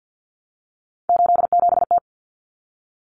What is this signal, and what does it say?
Morse code, heard as a keyed tone, about 50 wpm, 700 Hz. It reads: 87T
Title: Common Cut Numbers - 3-Digit 50wpm